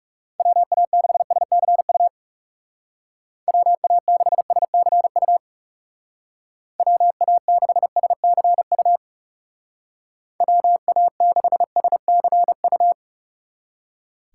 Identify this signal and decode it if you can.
Morse code, heard as a keyed tone, about 37 wpm, 700 Hz. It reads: WA6SCU WA6SCU WA6SCU WA6SCU
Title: Speed Racing - USA 2x3 Call Signs 30wpm